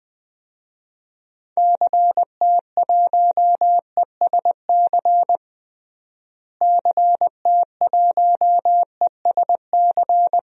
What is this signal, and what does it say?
Morse code, heard as a keyed tone, about 21 wpm, 700 Hz. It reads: CT1ESC CT1ESC
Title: DXCC-100 Call Signs - 6 Characters - Repeated 2x - 20wpm